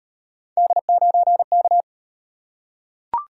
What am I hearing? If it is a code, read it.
Morse code, heard as a keyed tone, about 34 wpm, 700 Hz. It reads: D9K